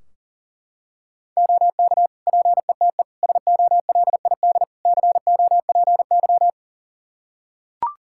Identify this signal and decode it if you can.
Morse code, heard as a keyed tone, about 40 wpm, 700 Hz. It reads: OK PETE SOLID COPY
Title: Practice QSO - Exchange 3 - More Info - 40wpm